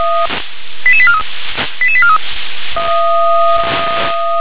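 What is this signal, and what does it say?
Five-tone selective call (ZVEI), heard as quick runs of five tones, tones of about 70 ms. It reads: 88843 88843
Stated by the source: ZVEI five-tone call received on SDR.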